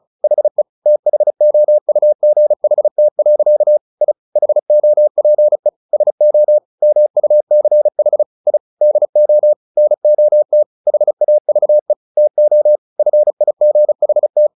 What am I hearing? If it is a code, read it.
Morse code, heard as a keyed tone, about 34 wpm, 600 Hz. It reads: HE THOUGHT. I HOPE SO MUCH I DO NOT HAVE TO FIGHT
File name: the old man and the sea 2 35wpm